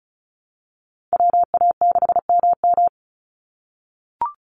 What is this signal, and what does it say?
Morse code, heard as a keyed tone, about 34 wpm, 700 Hz. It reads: WA6MM
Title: NA SOTA Chaser Callsigns 35wpm